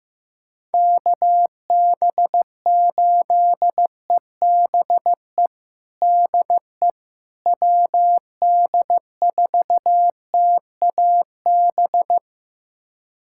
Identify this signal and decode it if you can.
Morse code, heard as a keyed tone, about 15 wpm, 700 Hz. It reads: KB8EBE DE WD4TAB
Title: USA Callsign DE Callsign - 6 Characters - 15wpm